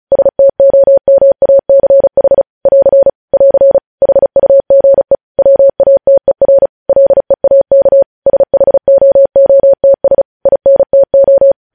Morse code, heard as a keyed tone, about 36 wpm, 550 Hz. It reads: STOMACH <AR> <AR> HUGE WATER LEAK SHOOTS INTO